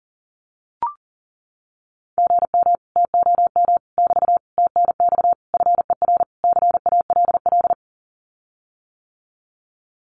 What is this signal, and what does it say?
Morse code, heard as a keyed tone, about 40 wpm, 700 Hz. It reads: GM TOM <BT> TNX FER CALL
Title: Common QSO Phrases Encore 40wpm